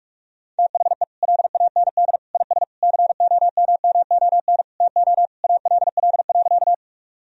Morse code, heard as a keyed tone, about 46 wpm, 700 Hz. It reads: THE LAND IS COMMON TO ALL.
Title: Sentences from Top 1000 Words - Review Part 3 - 45wpm